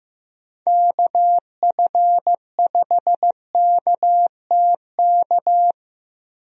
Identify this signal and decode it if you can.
Morse code, heard as a keyed tone, about 16 wpm, 700 Hz. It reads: KF5KTK
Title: USA Call Signs - 6 Characters - Repeated 3x - 15wpm